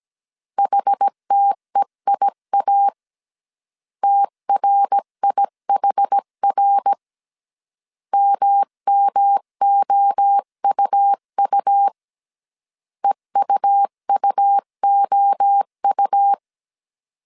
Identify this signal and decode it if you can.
Morse code, heard as a keyed tone, about 17 wpm, 800 Hz. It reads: HTEIA TRIHR MMOUU EUUOU
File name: CW / DET / 1 3 1500C S A 16wpm 2141